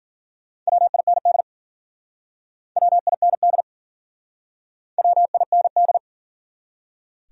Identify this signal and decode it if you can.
Morse code, heard as a keyed tone, about 52 wpm, 700 Hz. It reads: WIND WIND WIND